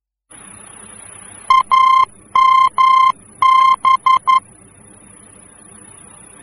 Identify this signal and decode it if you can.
Morse code, heard as a keyed tone, about 11 wpm, 1050 Hz. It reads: AMB
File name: france / mx / AMB-F-VOR